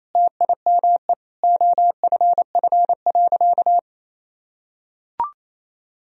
Morse code, heard as a keyed tone, about 28 wpm, 700 Hz. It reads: TIME OFF.
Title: Sentences from Top 1000 Words - Review Part 3 - 28wpm